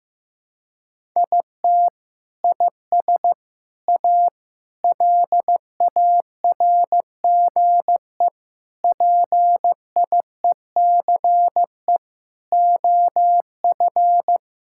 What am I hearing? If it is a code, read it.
Morse code, heard as a keyed tone, about 16 wpm, 700 Hz. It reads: IT IS A LARGE PIECE OF
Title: Sentences from Top 1000 Words - Review Part 5 - 15wpm